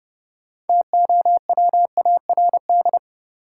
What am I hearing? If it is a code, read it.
Morse code, heard as a keyed tone, about 30 wpm, 700 Hz. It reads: TOWARD
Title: Sentences from Top 1000 Words - Review Part 6 - New Word Drill - 30wpm